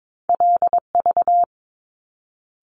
Morse code, heard as a keyed tone, about 22 wpm, 700 Hz. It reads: LV
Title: Top Two Letter Combinations in Words 22wpm